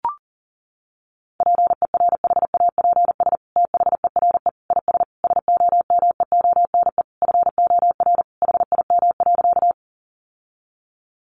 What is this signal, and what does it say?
Morse code, heard as a keyed tone, about 40 wpm, 700 Hz. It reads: PERHAPS THERE IS SOMEONE FOR HIM.
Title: Sentences from Top 600 Words 40wpm